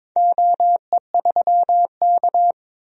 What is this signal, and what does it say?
Morse code, heard as a keyed tone, about 22 wpm, 700 Hz. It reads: OE3K
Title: DXCC-100 Call Signs - 4 Characters - Repeated 3x - 22wpm